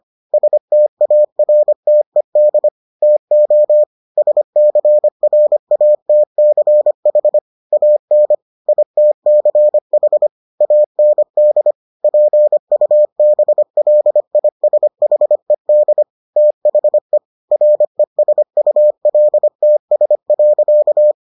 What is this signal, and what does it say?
Morse code, heard as a keyed tone, about 25 wpm, 600 Hz. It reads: STARTED TO SCRATCH AN ITCH AND PUBLISHED THE RESULTS.